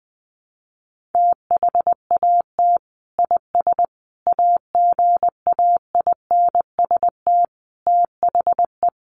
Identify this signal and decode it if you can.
Morse code, heard as a keyed tone, about 20 wpm, 700 Hz. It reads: THAT IS AGAINST THE
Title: Sentences from Top 1500 Words 20wpm